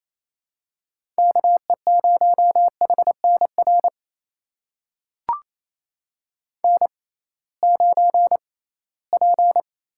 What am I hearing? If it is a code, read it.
Morse code, heard as a keyed tone, about 28 wpm, spaced as for 18 wpm, 700 Hz. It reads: KE0HNR N9P
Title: USA Call Signs - 6x Character Spacing - 28wpm